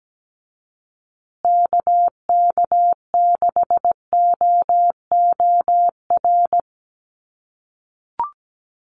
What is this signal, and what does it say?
Morse code, heard as a keyed tone, about 17 wpm, 700 Hz. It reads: KK6OOR
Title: USA Call Signs Repeated 3x 17wpm